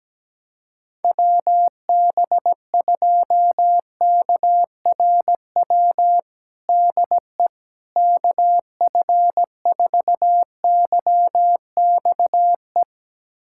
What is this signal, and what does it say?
Morse code, heard as a keyed tone, about 17 wpm, 700 Hz. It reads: WB2KRW DE KF4YXE
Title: USA Callsign DE Callsign - 6 Characters - 17wpm